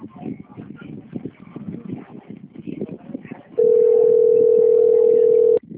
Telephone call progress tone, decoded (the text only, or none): ringback tone